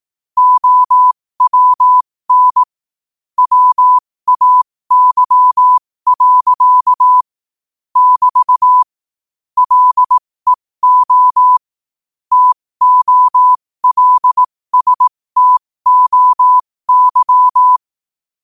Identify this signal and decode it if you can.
Morse code, heard as a keyed tone, about 18 wpm, spaced as for 15 wpm, 1000 Hz. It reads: OWN WAY. <BT> LEO TOLSTOY
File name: qotdfiles / 15wpm